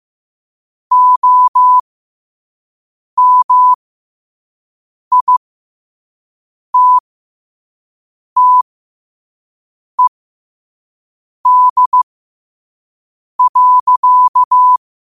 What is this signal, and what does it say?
Morse code, heard as a keyed tone, about 15 wpm, spaced as for 5 wpm, 1000 Hz. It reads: OMITTED.
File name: qotdfiles / 5wpm